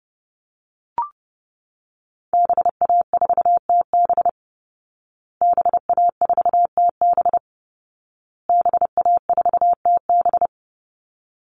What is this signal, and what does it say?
Morse code, heard as a keyed tone, about 30 wpm, 700 Hz. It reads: BA4TB BA4TB BA4TB
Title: Call Signs Repeated 2x - 30wpm